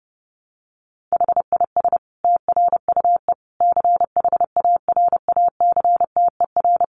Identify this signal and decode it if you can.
Morse code, heard as a keyed tone, about 28 wpm, 700 Hz. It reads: HIS TRUE CHARACTER
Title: Sentences from Top 1200 Words 30wpm